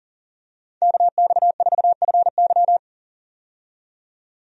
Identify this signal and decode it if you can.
Morse code, heard as a keyed tone, about 40 wpm, 700 Hz. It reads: KX4FY
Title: USA Call Signs - 5 Characters - 40wpm